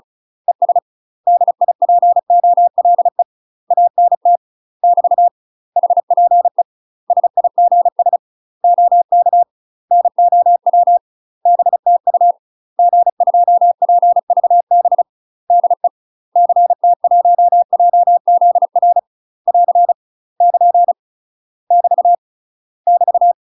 Morse code, heard as a keyed tone, about 36 wpm, 700 Hz. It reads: ES DIPOLE ANT <BT> HPE SIGS OK NOW BTU G2PVB DE CT1JZR <AR> <KN> <BT> <BT>
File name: UK MASTER QSOs TRAINING / 35WPM / UK MASTER QSOs 0023